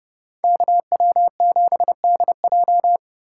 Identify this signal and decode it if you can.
Morse code, heard as a keyed tone, about 28 wpm, 700 Hz. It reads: KW7DJ
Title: Speed Racing - USA 2x2 Call Signs 22wpm